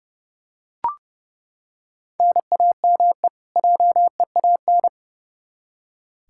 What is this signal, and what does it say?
Morse code, heard as a keyed tone, about 30 wpm, 700 Hz. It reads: NAME JEAN
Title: Practice QSO - Exchange 1 - Beginning - 30wpm